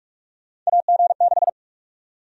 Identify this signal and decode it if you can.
Morse code, heard as a keyed tone, about 46 wpm, 700 Hz. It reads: AG6
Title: USA Call Sign Prefixes - 45wpm